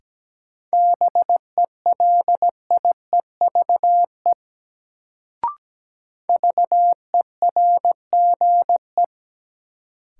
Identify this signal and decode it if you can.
Morse code, heard as a keyed tone, about 17 wpm, 700 Hz. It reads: BELIEVE VERGE
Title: Sets of 4 Words - New Word Drill - 17wpm